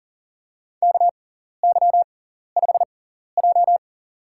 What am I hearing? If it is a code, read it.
Morse code, heard as a keyed tone, about 40 wpm, spaced as for 14 wpm, 700 Hz. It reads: KY5J
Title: USA Call Signs - 4 Characters - 6x Character Spacing - 40wpm